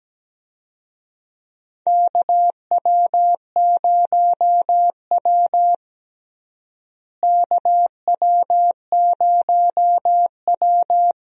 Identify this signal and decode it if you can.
Morse code, heard as a keyed tone, about 17 wpm, 700 Hz. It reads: KW0W KW0W
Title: USA Call Signs - 4 Characters - Repeated 2x - 17wpm